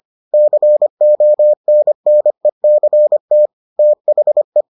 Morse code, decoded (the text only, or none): CONNECT THE